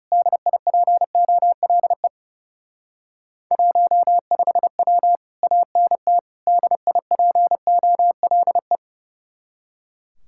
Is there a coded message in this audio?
Yes: Morse code from 0.1 to 8.8 s, heard as a keyed tone, about 38 wpm, 700 Hz. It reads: DIPOLE 15W ANT DIPOLE